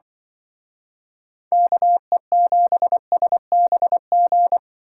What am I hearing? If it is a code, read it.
Morse code, heard as a keyed tone, about 24 wpm, 700 Hz. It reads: KE7SBG
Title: Speed Racing - USA 2x3 Call Signs 20wpm